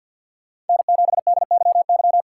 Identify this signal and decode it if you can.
Morse code, heard as a keyed tone, about 51 wpm, 700 Hz. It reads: N7DYY
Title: USA Call Signs - 5 Characters - 2x Character Spacing - 50wpm